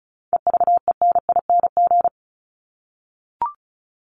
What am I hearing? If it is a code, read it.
Morse code, heard as a keyed tone, about 34 wpm, 700 Hz. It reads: EVENING
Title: Top 1501-1750 Words 35wpm